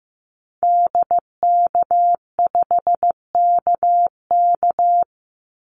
Morse code, heard as a keyed tone, about 15 wpm, 700 Hz. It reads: DK5KK
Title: Call Signs Repeated 2x - 15wpm